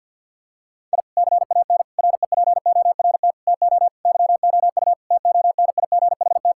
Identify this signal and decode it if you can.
Morse code, heard as a keyed tone, about 50 wpm, 700 Hz. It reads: I CAN REPORT TO YOU TONIGHT
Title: Sentences from Top 1000 Words - Review Part 6 - 50wpm